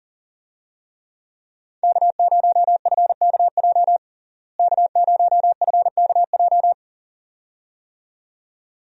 Morse code, heard as a keyed tone, about 40 wpm, 700 Hz. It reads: K0FKJ K0FKJ
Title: USA Call Signs - 5 Characters - Repeated 2x - 40wpm